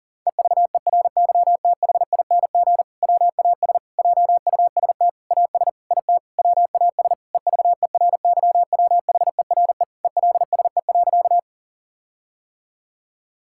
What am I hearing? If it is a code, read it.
Morse code, heard as a keyed tone, about 40 wpm, 700 Hz. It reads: EVERYTHING WAS JUST AS IT WAS EVERYWHERE ELSE.
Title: Sentences from Top 1600 Words 40wpm